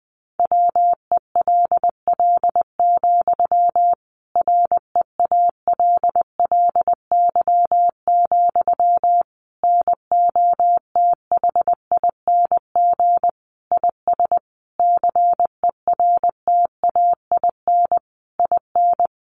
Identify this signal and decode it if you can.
Morse code, heard as a keyed tone, about 20 wpm, 700 Hz. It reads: WELL, REALLY, NOTHING IS CERTAIN IN
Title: Sentences from Top 600 Words 20wpm